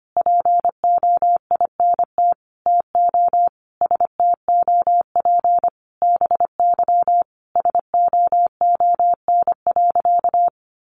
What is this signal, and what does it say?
Morse code, heard as a keyed tone, about 25 wpm, 700 Hz. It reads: POINT TO STOP BY SOON.
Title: Sentences from Top 1500 Words 25wpm